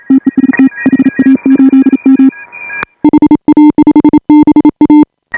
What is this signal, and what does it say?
Morse code, heard as a keyed tone, about 34 wpm, 310 Hz. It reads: TEST HA8M HA5BA